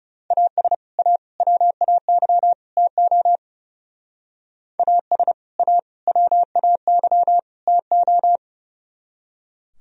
Morse code, heard as a keyed tone, about 37 wpm, 700 Hz. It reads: AS A WAY TO AS A WAY TO
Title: Speed-Racing - Sets of 4 Words 30wpm